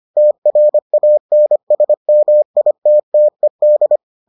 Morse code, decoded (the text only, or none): TRANSMITTED